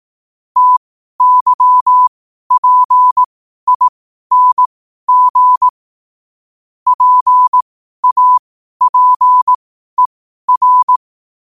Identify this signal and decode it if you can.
Morse code, heard as a keyed tone, about 18 wpm, spaced as for 12 wpm, 1000 Hz. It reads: TYPING PAPER
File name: 12wpm